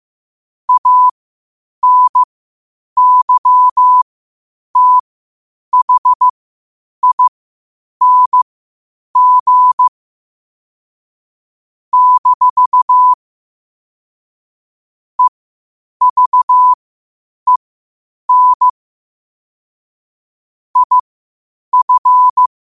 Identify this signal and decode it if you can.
Morse code, heard as a keyed tone, about 15 wpm, spaced as for 8 wpm, 1000 Hz. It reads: ANYTHING - EVEN IF
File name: qotdfiles / 8wpm